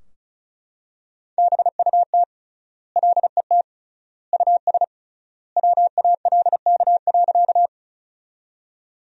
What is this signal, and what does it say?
Morse code, heard as a keyed tone, about 36 wpm, 700 Hz. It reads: BUT LET US WALK.
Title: 3x Word-Spacing - Sentences from Top 300 Words - 35wpm